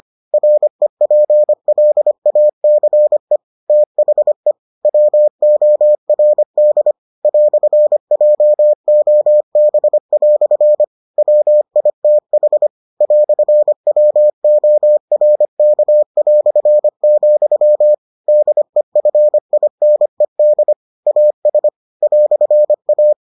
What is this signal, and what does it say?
Morse code, heard as a keyed tone, about 25 wpm, 600 Hz. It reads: REPLACE THE WORD "JOB" WITH "WORK", DEFINED AS "A